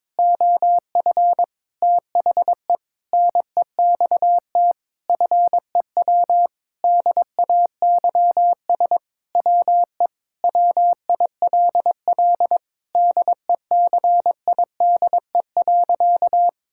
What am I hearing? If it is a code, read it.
Morse code, heard as a keyed tone, about 21 wpm, 700 Hz. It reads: OF THE NEXT FEW DAYS WE WILL DECIDE.
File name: Sentences with 7-Word Phrases - 22wpm